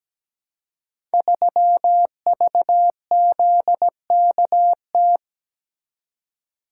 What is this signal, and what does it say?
Morse code, heard as a keyed tone, about 17 wpm, 690 Hz. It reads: 3VZKT
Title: Code Groups - 5 Letter:Numbers - 17wpm